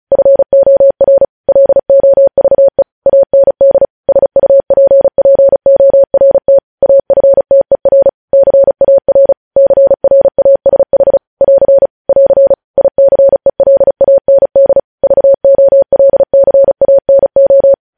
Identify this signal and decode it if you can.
Morse code, heard as a keyed tone, about 36 wpm, 550 Hz. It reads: FOR LOVE AND SUPPORT AFTER CAR CRASH <AR> <AR> ICELAND VOLCANO
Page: news headlines in morse code at 35 wpm